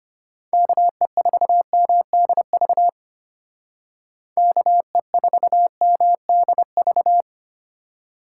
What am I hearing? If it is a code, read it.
Morse code, heard as a keyed tone, about 30 wpm, 700 Hz. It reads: KE4MDV KE4MDV
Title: Speed Racing - USA 2x3 Call Signs 25wpm